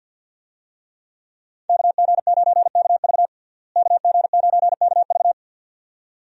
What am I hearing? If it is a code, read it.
Morse code, heard as a keyed tone, about 50 wpm, 700 Hz. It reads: KG9KV KG9KV
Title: USA Call Signs - 5 Characters - Repeated 2x - 50wpm